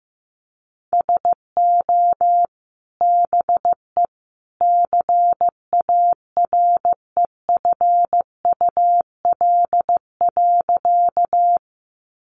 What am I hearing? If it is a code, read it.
Morse code, heard as a keyed tone, about 15 wpm, 700 Hz. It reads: SO BE CAREFUL.
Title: A Sound Of Thunder 15wpm